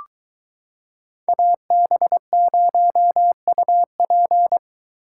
Morse code, heard as a keyed tone, about 22 wpm, 700 Hz. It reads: AB0UP